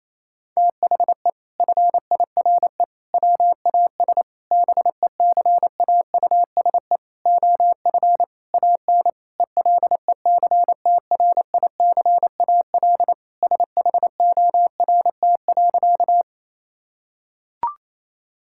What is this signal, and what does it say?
Morse code, heard as a keyed tone, about 27 wpm, 700 Hz. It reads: THE FIRE WAS BECAUSE OF AN ELECTRICAL SHORT.
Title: Sentences from Top 1000 Words - Review Part 6 - 28wpm